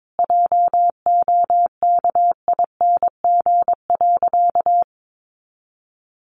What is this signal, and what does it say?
Morse code, heard as a keyed tone, about 22 wpm, 700 Hz. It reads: JOKING.
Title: A Sound Of Thunder 22wpm